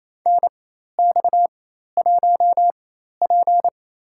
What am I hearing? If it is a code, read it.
Morse code, heard as a keyed tone, about 28 wpm, spaced as for 13 wpm, 700 Hz. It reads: NX1P
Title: USA Call Signs - 4 Characters - 4x Character Spacing - 28wpm